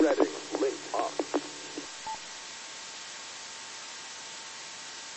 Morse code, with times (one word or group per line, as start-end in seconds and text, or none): none